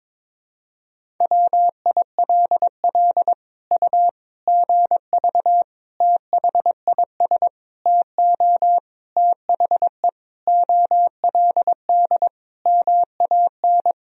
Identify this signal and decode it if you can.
Morse code, heard as a keyed tone, about 22 wpm, 700 Hz. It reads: WILL U GV THIS TO THE OLD MAN
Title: Sentences from Top 200 Words and Abbreviations 22wpm